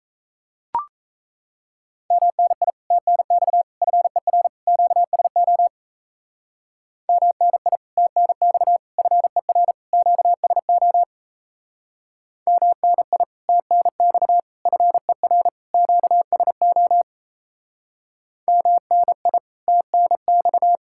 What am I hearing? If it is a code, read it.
Morse code, heard as a keyed tone, about 35 wpm, 700 Hz. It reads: MNI TNX FER QSO MNI TNX FER QSO MNI TNX FER QSO MNI TNX
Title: Speed-Racing Common QSO Phrases Encore - 28wpm